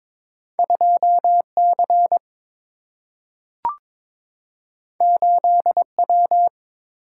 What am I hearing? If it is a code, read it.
Morse code, heard as a keyed tone, about 22 wpm, 700 Hz. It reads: 2 C 8 W
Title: Single-Letter Single-Number 22wpm